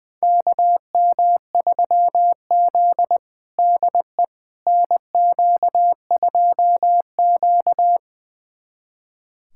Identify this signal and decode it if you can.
Morse code, heard as a keyed tone, about 20 wpm, 700 Hz. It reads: KM3Z DE NQ2Q
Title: USA Callsign DE Callsign - 4 Characters - 20wpm